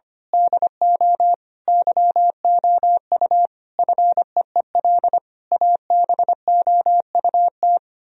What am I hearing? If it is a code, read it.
Morse code, heard as a keyed tone, about 24 wpm, 700 Hz. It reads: DO YOU FEEL ABOUT
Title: Speed-Racing - Sets of 5 Words 25wpm